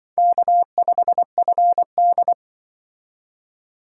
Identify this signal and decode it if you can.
Morse code, heard as a keyed tone, about 24 wpm, 700 Hz. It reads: K5FD
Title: Speed Racing - USA 1x2 Call Signs 20wpm